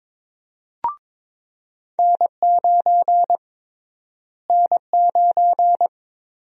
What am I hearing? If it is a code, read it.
Morse code, heard as a keyed tone, about 21 wpm, 700 Hz. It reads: N9 N9
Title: USA Call Sign Prefixes - Repeated 2x - 22wpm